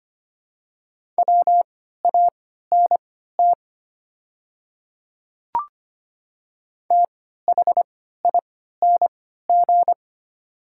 Morse code, heard as a keyed tone, about 25 wpm, spaced as for 14 wpm, 700 Hz. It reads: WANT THING
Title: 3x Character Spacing - Top 200 Words - 25wpm